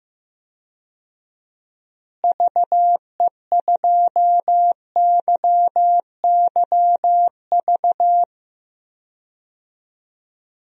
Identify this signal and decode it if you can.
Morse code, heard as a keyed tone, about 15 wpm, 700 Hz. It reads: VE2YYV
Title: DXCC-100 Call Signs - 6 Characters - Repeated 2x - 15wpm